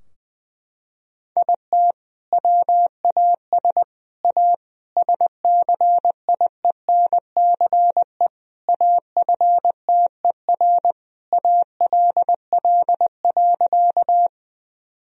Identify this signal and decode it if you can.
Morse code, heard as a keyed tone, about 20 wpm, 700 Hz. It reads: IT WAS A SCIENCE AFTER ALL.
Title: Sentences from Top 1000 Words - Review Part 2 - 20wpm